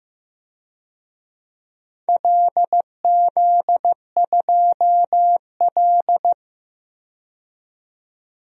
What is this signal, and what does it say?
Morse code, heard as a keyed tone, about 15 wpm, 700 Hz. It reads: LZ2L
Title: DXCC-100 Call Signs - 4 Characters - 15wpm